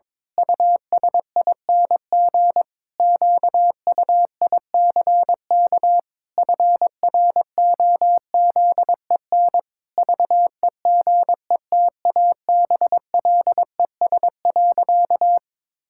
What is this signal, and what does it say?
Morse code, heard as a keyed tone, about 22 wpm, 700 Hz. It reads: USING QUICK FROZEN VEGETABLES.
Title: Pangram Challenge v2 - 22wpm